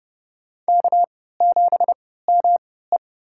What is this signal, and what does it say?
Morse code, heard as a keyed tone, about 30 wpm, spaced as for 17 wpm, 700 Hz. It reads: K7ME
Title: USA Call Signs - 3x Character Spacing - 30wpm